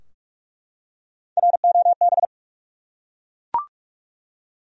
Morse code, heard as a keyed tone, about 48 wpm, 700 Hz. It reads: ROB
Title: USA Common Names - Max 3 Letters - 45wpm